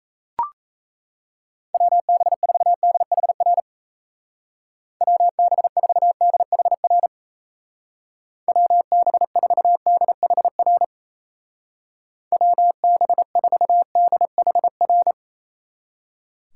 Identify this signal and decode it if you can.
Morse code, heard as a keyed tone, about 34 wpm, 700 Hz. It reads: WB4DHR WB4DHR WB4DHR WB4DHR
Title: Speed Racing - USA 2x3 Call Signs 28wpm